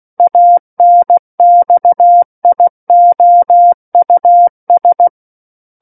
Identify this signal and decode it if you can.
Morse code, heard as a keyed tone, about 16 wpm, 700 Hz. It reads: ANXIOUS
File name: Book One / 07-Chapter 7 - 16 WPM